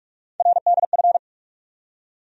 Morse code, heard as a keyed tone, about 42 wpm, 700 Hz. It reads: RDF
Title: USA Call Sign Postfixes - 3 Letters 45wpm